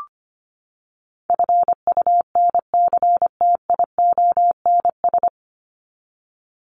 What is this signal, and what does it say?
Morse code, heard as a keyed tone, about 25 wpm, 700 Hz. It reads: FUNCTIONS
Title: ES:S Suffix - Top 100 Words with Base Words 25wpm